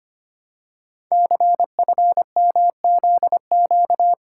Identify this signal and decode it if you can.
Morse code, heard as a keyed tone, about 25 wpm, 700 Hz. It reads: CFMZQ
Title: Code Groups - 5 Letters - 25wpm